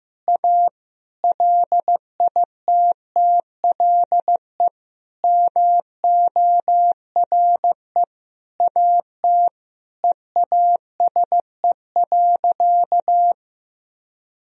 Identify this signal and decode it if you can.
Morse code, heard as a keyed tone, about 15 wpm, 700 Hz. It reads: A LITTLE MORE AT EASE.
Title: Sentences from Top 1000 Words - Review Part 1 - 15wpm